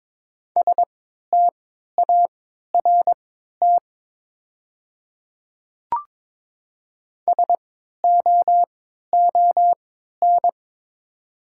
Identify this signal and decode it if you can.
Morse code, heard as a keyed tone, about 23 wpm, spaced as for 13 wpm, 690 Hz. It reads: START SOON
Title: 3x Character Spacing - Top 500 Words - 22wpm